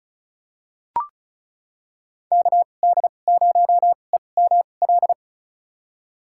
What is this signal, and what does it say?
Morse code, heard as a keyed tone, about 34 wpm, spaced as for 25 wpm, 700 Hz. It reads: KD0EML
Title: USA Call Signs - 6 Characters - 2x Character Spacing - 35wpm